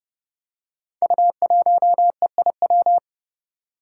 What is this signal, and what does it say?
Morse code, heard as a keyed tone, about 30 wpm, 700 Hz. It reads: U1EIW
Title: Five Challenging Letter-Number Combinations 30wpm